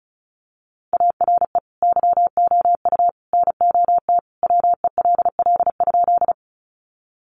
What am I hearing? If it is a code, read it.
Morse code, heard as a keyed tone, about 34 wpm, 700 Hz. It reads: ARE YOU NOT WELL?
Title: Sentences from Top 600 Words 35wpm